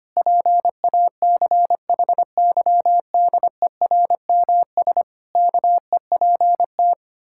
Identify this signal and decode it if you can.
Morse code, heard as a keyed tone, about 25 wpm, 700 Hz. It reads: PACHYDERMS KEPT
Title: Pangram Challenge v2 - 25wpm